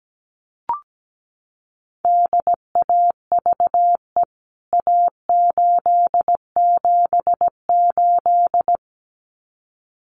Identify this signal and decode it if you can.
Morse code, heard as a keyed tone, about 17 wpm, 700 Hz. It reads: DAVE A878
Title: CWT Exchange 17wpm